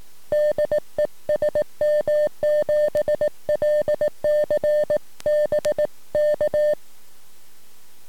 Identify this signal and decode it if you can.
Morse code, heard as a keyed tone, about 17 wpm, 600 Hz. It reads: DE SM7LCB K
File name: sm7lcb / 15-sm7eyw 23cm